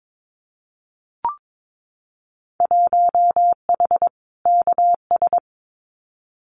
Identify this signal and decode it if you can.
Morse code, heard as a keyed tone, about 22 wpm, 700 Hz. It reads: 1H KS
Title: Winter Field Day Exchanges 22wpm